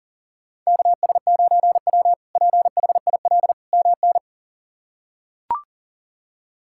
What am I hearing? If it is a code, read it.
Morse code, heard as a keyed tone, about 40 wpm, 700 Hz. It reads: KS9W PHIL MN
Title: K1USN SST QSO Practice - 40wpm